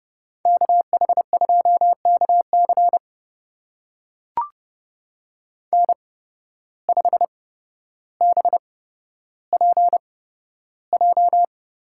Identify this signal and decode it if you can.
Morse code, heard as a keyed tone, about 30 wpm, spaced as for 13 wpm, 700 Hz. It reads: KH2KC N5BPJ
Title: USA Call Signs - 5 Characters - 8x Character Spacing - 30wpm